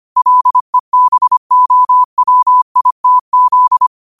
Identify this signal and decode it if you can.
Morse code, heard as a keyed tone, about 25 wpm, 1000 Hz. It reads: LEBOWITZ
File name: qotdfiles / 25wpm